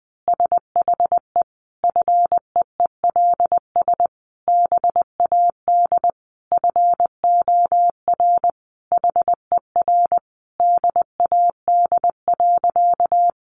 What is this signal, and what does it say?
Morse code, heard as a keyed tone, about 20 wpm, 700 Hz. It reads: SHE FEELS BAD FOR HER DAD.
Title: Additional Easy Phrases 20wpm